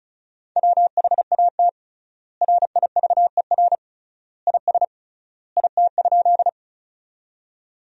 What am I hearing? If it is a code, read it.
Morse code, heard as a keyed tone, about 36 wpm, 700 Hz. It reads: WHAT RIVER IS IT?
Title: 3x Word-Spacing - Sentences from Top 300 Words - 35wpm